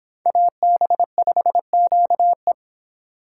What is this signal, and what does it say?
Morse code, heard as a keyed tone, about 24 wpm, 700 Hz. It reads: AB5QE